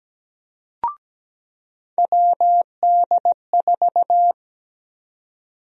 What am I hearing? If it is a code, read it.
Morse code, heard as a keyed tone, about 17 wpm, 700 Hz. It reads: WD4
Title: USA Call Sign Prefixes - 17wpm